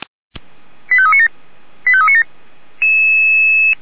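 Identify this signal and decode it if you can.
Five-tone selective call (ZVEI), heard as quick runs of five tones, tones of about 70 ms. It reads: 85397 85397
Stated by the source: ZVEI five-tone selective call received on SDR.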